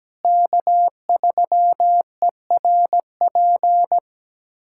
Morse code, heard as a keyed tone, about 17 wpm, 700 Hz. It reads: K3ERP
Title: USA Callsign DE Callsign 17wpm